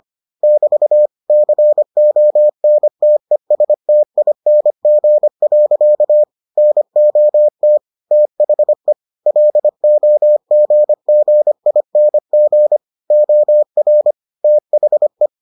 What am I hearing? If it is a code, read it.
Morse code, heard as a keyed tone, about 25 wpm, 600 Hz. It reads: <BT> CONTESTING. NOT THE LOGGING OR THE